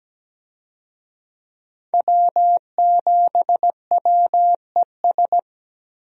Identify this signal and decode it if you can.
Morse code, heard as a keyed tone, about 17 wpm, 700 Hz. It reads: W7WES
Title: USA Call Signs - 5 Characters - Repeated 2x - 17wpm